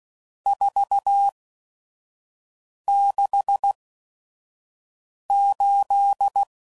Morse code, heard as a keyed tone, about 16 wpm, spaced as for 5 wpm, 800 Hz. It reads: 468
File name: CW Code Groups @5WPM / RST's / Various RST No's